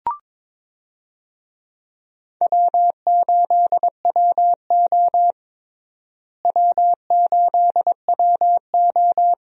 Morse code, heard as a keyed tone, about 22 wpm, 700 Hz. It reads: W8WO W8WO
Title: USA Call Signs - 4 Characters - Repeated 2x - 22wpm